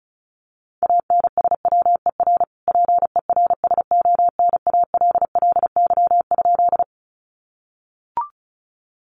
Morse code, heard as a keyed tone, about 34 wpm, 700 Hz. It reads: ANSWER PERSONALLY?
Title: Sentences from Top 1400 Words 35wpm